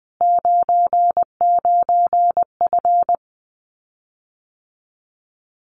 Morse code, heard as a keyed tone, about 20 wpm, 700 Hz. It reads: 99F
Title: Common QSO Phrases Encore 20wpm